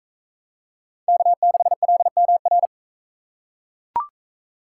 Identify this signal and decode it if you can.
Morse code, heard as a keyed tone, about 42 wpm, 700 Hz. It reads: K6LMR